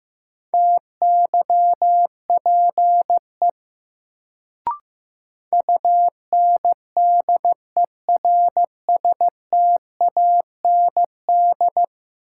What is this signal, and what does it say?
Morse code, heard as a keyed tone, about 15 wpm, 700 Hz. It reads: TYPE UNDERSTAND
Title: Sentences from Top 600 Words - New Word Drill - 15wpm